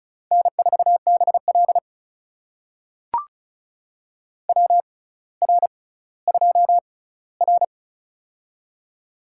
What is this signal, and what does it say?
Morse code, heard as a keyed tone, about 36 wpm, spaced as for 18 wpm, 700 Hz. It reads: N4BL WR2R
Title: USA Call Signs - 4 Characters - 6x Character Spacing - 35wpm